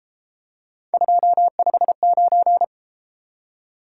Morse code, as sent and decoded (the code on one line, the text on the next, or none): ..--- ..... ----.
259